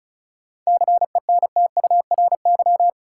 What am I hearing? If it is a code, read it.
Morse code, heard as a keyed tone, about 36 wpm, 700 Hz. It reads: CENTURY
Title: Sets of 5 Words - New Word Drill - 35wpm